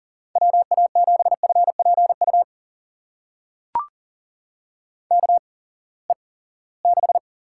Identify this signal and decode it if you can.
Morse code, heard as a keyed tone, about 39 wpm, spaced as for 22 wpm, 700 Hz. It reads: WA7FPU KE6
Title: USA Call Signs - 6 Characters - 8x Character Spacing - 40wpm